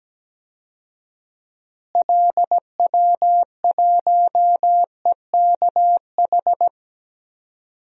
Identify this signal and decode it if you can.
Morse code, heard as a keyed tone, about 17 wpm, 700 Hz. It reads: LW1EKH
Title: DXCC-100 Call Signs - 6 Characters - Repeated 2x - 17wpm